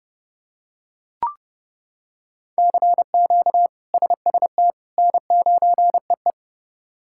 Morse code, heard as a keyed tone, about 30 wpm, 700 Hz. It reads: CQ SST N9EE
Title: K1USN SST QSO Practice - 30wpm